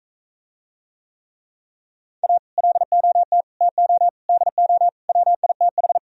Morse code, heard as a keyed tone, about 41 wpm, 700 Hz. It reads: A LOT TO DO WITH